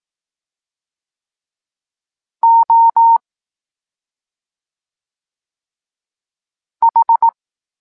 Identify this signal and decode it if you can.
Morse code, heard as a keyed tone, about 18 wpm, 920 Hz. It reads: O H
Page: Tekst 14 – doskonalenie odbioru poznanych liter